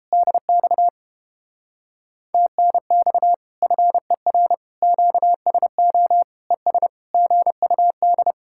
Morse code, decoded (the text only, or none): DX TNX FER QSO ES GUD